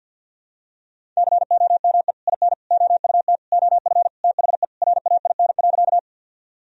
Morse code, heard as a keyed tone, about 48 wpm, 700 Hz. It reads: COME IN OUT OF THE RAIN.
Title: Sentences from Top 1000 Words - Review Part 1 - 50wpm